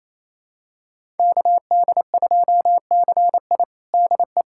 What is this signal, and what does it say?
Morse code, heard as a keyed tone, about 28 wpm, 700 Hz. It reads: KD2CI DE
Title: USA Callsign DE Callsign - 5 Characters - 28wpm